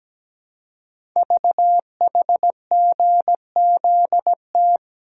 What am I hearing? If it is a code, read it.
Morse code, heard as a keyed tone, about 17 wpm, 700 Hz. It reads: VHGZT
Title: Code Groups - 5 Letters - 17wpm